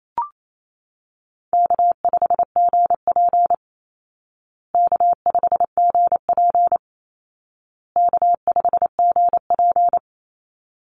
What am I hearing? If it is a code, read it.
Morse code, heard as a keyed tone, about 28 wpm, 700 Hz. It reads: K5GP K5GP K5GP
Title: Easy Call Signs Repeated 3x - 28wpm